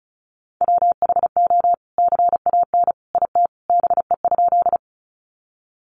Morse code, heard as a keyed tone, about 34 wpm, 700 Hz. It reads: WHO CAN IT BE?
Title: Sentences from Top 100 Words 35wpm